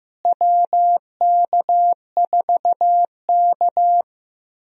Morse code, heard as a keyed tone, about 15 wpm, 700 Hz. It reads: WK4K
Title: USA Call Signs - 4 Characters - Repeated 3x - 15wpm